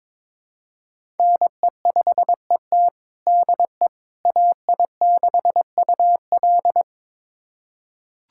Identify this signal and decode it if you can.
Morse code, heard as a keyed tone, about 22 wpm, 700 Hz. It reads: NE5ET DE AI6UL
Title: USA Callsign DE Callsign - 5 Characters - 22wpm